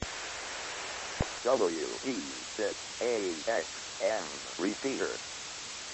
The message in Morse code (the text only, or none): none